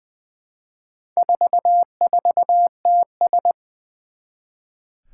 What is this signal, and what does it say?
Morse code, heard as a keyed tone, about 20 wpm, 700 Hz. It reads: 44TS
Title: Four Challenging Letter-Number Combinations 20wpm